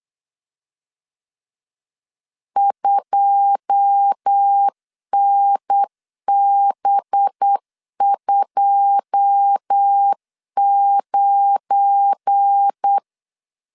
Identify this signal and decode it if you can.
Morse code, heard as a keyed tone, about 8 wpm, 800 Hz. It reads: 2NB29
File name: LEZIONI / 9 2 8wpm 3437